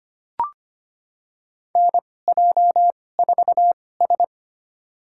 Morse code, heard as a keyed tone, about 25 wpm, spaced as for 18 wpm, 700 Hz. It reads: NJ4S
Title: USA Call Signs - 2x Character Spacing - 25wpm